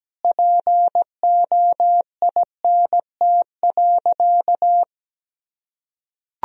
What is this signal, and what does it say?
Morse code, heard as a keyed tone, about 17 wpm, 700 Hz. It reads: POINT.
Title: Sentences from Top 1000 Words - Review Part 1 - 17wpm